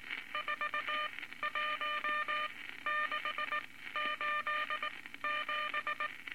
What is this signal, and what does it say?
Morse code, heard as a keyed tone, about 19 wpm, spaced as for 14 wpm, 1200 Hz, with noise. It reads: 41687
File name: Morse Code Records / 19 A LCM 166 Ten Words Per Minute